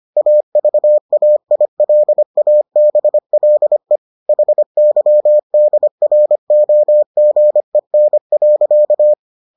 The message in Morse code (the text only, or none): AVAILABLE HYDROGEN.